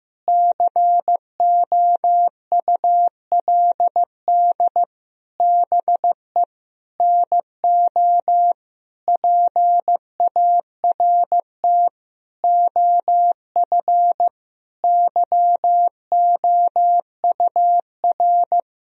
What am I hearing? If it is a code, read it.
Morse code, heard as a keyed tone, about 15 wpm, 700 Hz. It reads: COULD BE NO PART OF YOUR
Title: Sentences from Top 1000 Words - Review Part 2 - 15wpm